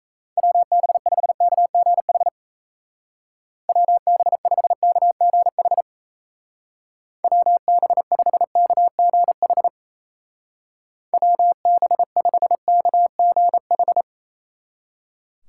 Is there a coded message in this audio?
Yes: Morse code from 0.4 to 14.0 s, heard as a keyed tone, about 37 wpm, 700 Hz. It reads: WB5KGH WB5KGH WB5KGH WB5KGH